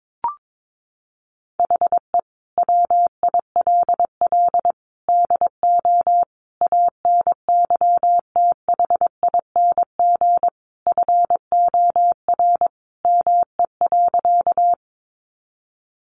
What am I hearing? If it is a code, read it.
Morse code, heard as a keyed tone, about 22 wpm, 700 Hz. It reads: HE WILL DO ANYTHING FOR ME.
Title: Sentences from Top 1000 Words 22wpm